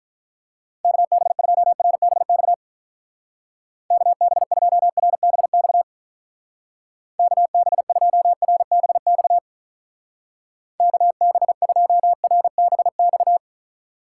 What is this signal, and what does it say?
Morse code, heard as a keyed tone, about 47 wpm, 700 Hz. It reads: KB2RBX KB2RBX KB2RBX KB2RBX